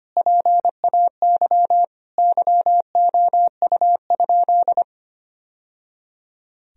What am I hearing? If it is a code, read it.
Morse code, heard as a keyed tone, about 25 wpm, 700 Hz. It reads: PAY YOU?
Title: Sentences from Top 1000 Words - Review Part 3 - 25wpm